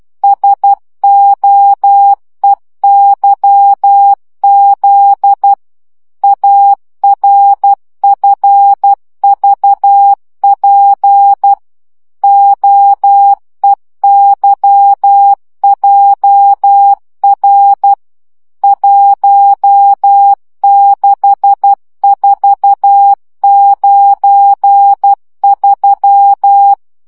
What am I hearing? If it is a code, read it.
Morse code, heard as a keyed tone, about 12 wpm, 800 Hz. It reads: SOEYZ ARFVP OEYJR 16493
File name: morze vizsga 2011-2 10